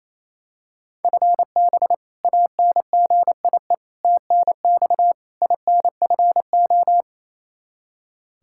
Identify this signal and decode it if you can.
Morse code, heard as a keyed tone, about 28 wpm, 700 Hz. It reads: FB ANGIE TNX INFO
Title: K1USN SST QSO Practice 28wpm